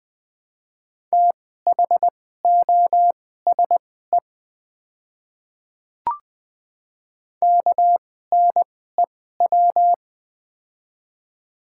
Morse code, heard as a keyed tone, about 21 wpm, spaced as for 15 wpm, 700 Hz. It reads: THOSE KNEW
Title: 2x Character Spacing - Top 500 Words - 20wpm